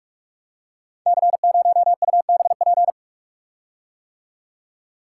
Morse code, heard as a keyed tone, about 44 wpm, 700 Hz. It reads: C0UBP
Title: Code Groups - 5 Letter:Numbers - 45wpm